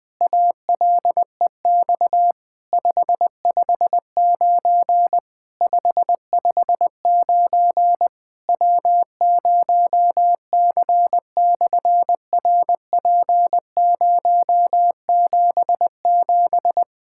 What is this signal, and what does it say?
Morse code, heard as a keyed tone, about 20 wpm, 700 Hz. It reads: ALEX 559 559 W0C/RP077
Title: SOTA QSO Practice 20wpm